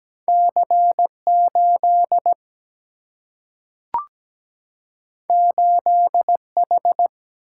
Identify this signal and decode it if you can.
Morse code, heard as a keyed tone, about 17 wpm, 700 Hz. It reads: C8 8H
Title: Single-Letter Single-Number 17wpm